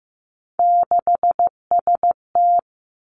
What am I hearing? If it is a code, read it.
Morse code, heard as a keyed tone, about 15 wpm, 700 Hz. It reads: 6ST
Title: Three Challenging Letter-Number Combinations 15wpm